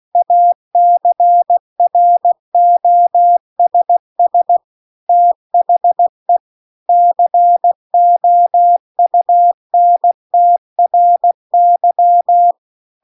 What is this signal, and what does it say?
Morse code, heard as a keyed tone, about 16 wpm, 700 Hz. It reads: ACROSS THE COUNTRY